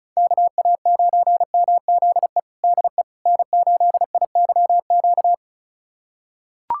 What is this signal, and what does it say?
Morse code, heard as a keyed tone, about 36 wpm, 700 Hz. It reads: KA9MZE DE N8IYQ
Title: USA Callsign DE Callsign 35wpm